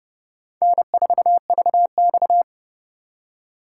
Morse code, heard as a keyed tone, about 37 wpm, 700 Hz. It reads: N4VX
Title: Speed Racing - USA Call Signs 30wpm